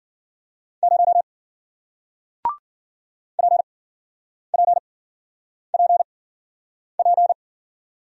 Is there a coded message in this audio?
Yes: Morse code from 0.8 to 7.3 s, heard as a keyed tone, about 53 wpm, spaced as for 9 wpm, 700 Hz. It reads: 0 PPPP